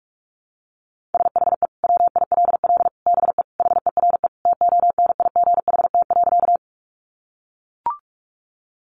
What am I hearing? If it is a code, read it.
Morse code, heard as a keyed tone, about 44 wpm, 700 Hz. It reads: SHE WILL BE HERE TONIGHT.
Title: Sentences from Top 600 Words 45wpm